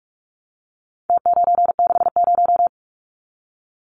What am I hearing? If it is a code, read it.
Morse code, heard as a keyed tone, about 44 wpm, 700 Hz. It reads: T960
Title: Numbers - Contest Style - 4-Digit 45wpm